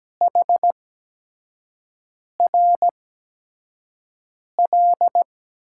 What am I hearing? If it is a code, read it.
Morse code, heard as a keyed tone, about 17 wpm, spaced as for 5 wpm, 700 Hz. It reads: HRL
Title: USA Call Signs - 6 Characters - 8x Character Spacing - 17wpm